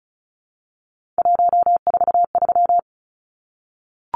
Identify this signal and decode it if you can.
Morse code, heard as a keyed tone, about 34 wpm, 700 Hz. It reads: 143
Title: Numbers - 3-Digit 35wpm